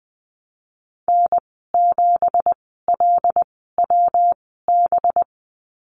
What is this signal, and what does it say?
Morse code, heard as a keyed tone, about 20 wpm, spaced as for 15 wpm, 700 Hz. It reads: N7LWB
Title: 186 Lesson 30g - Letter b - Call Signs 20wpm